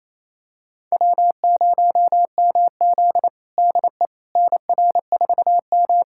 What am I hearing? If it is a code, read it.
Morse code, heard as a keyed tone, about 28 wpm, 700 Hz. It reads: W0MZ DE NR4M
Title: USA Callsign DE Callsign - 4 Characters - 28wpm